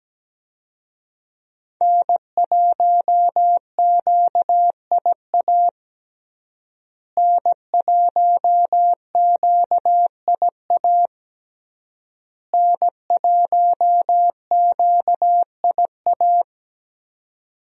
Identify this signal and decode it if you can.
Morse code, heard as a keyed tone, about 17 wpm, 700 Hz. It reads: N1QIA N1QIA N1QIA
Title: USA Call Signs - 5 Characters - Repeated 3x - 17wpm